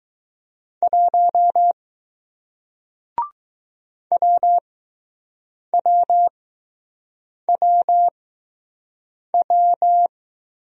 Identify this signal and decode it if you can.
Morse code, heard as a keyed tone, about 20 wpm, spaced as for 6 wpm, 700 Hz. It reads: 1 WWWW